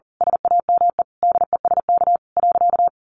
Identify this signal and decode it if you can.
Morse code, heard as a keyed tone, about 40 wpm, 700 Hz. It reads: SAME DESK.
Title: A Sound Of Thunder 40wpm